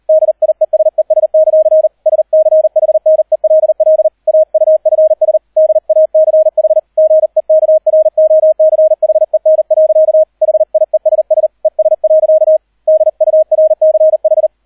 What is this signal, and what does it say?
Morse code, heard as a keyed tone, about 38 wpm, 630 Hz, with radio noise. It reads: DIESES; SCHNELL AUFS DACH GEKROCHEN. HIESS ES. DURCH